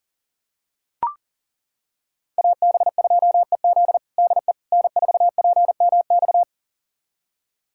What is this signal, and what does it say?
Morse code, heard as a keyed tone, about 40 wpm, 700 Hz. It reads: AB2EZ DE N4PMX
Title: USA Callsign DE Callsign - 5 Characters - 40wpm